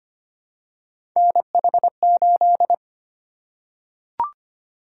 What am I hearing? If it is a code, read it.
Morse code, heard as a keyed tone, about 25 wpm, 700 Hz. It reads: NH8
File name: Three Letter-Number Combinations - 25wpm